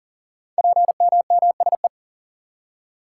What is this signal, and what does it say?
Morse code, heard as a keyed tone, about 39 wpm, 700 Hz. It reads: PMMSE
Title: Code Groups - 5 Letters - 40wpm